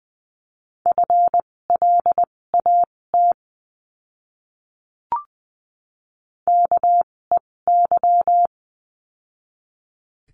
Morse code, heard as a keyed tone, about 20 wpm, spaced as for 16 wpm, 700 Hz. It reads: FLAT KEY
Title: 256 Lesson 42a - Words 12-20wpm